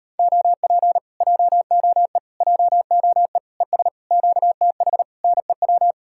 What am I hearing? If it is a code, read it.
Morse code, heard as a keyed tone, about 33 wpm, 700 Hz. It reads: OP JOE JOE ES QTH NEW